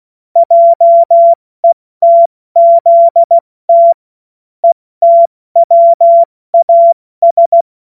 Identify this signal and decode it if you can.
Morse code, heard as a keyed tone, about 16 wpm, 680 Hz. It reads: JETZT ETWAS
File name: Morsekurs / Witze / Witz 28.November 2023 70 F80